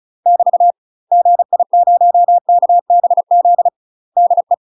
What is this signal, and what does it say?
Morse code, heard as a keyed tone, about 34 wpm, 700 Hz. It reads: <BT> GI0KBZ DE
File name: UK MASTER QSOs TRAINING/35WPM